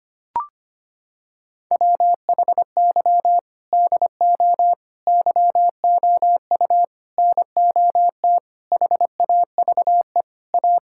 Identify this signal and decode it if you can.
Morse code, heard as a keyed tone, about 25 wpm, 700 Hz. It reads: WHY DO YOU NOT HAVE A
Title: Sentences from Top 1600 Words 25wpm